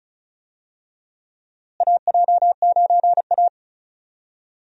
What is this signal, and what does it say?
Morse code, heard as a keyed tone, about 34 wpm, 700 Hz. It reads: AJ9A
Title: USA Call Signs - 4 Characters - Repeated 3x - 35wpm